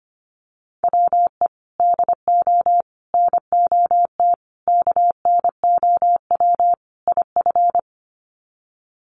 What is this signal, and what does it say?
Morse code, heard as a keyed tone, about 25 wpm, 700 Hz. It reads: WE DO NOT KNOW IF
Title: Sets of 5 Words Encore2 25wpm